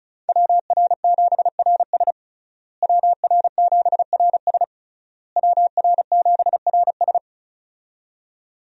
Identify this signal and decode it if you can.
Morse code, heard as a keyed tone, about 36 wpm, 700 Hz. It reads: WR7RS WR7RS WR7RS
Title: USA Call Signs - 5 Characters - Repeated 3x - 35wpm